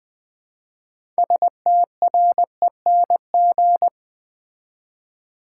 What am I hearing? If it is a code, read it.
Morse code, heard as a keyed tone, about 20 wpm, 700 Hz. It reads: STRENG
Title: Build A Word - 2 Letters At A Time - 8-Letter Words - 20wpm